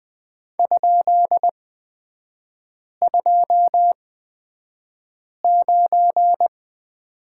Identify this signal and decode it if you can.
Morse code, heard as a keyed tone, about 20 wpm, spaced as for 5 wpm, 700 Hz. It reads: ?29
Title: 255 Lesson 41e - Period - Three Characters 20wpm